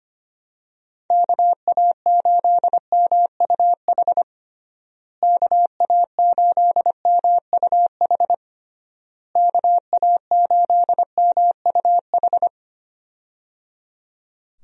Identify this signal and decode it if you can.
Morse code, heard as a keyed tone, about 25 wpm, 700 Hz. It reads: KA8MUH KA8MUH KA8MUH
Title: USA Call Signs - 6 Characters - Repeated 3x - 25wpm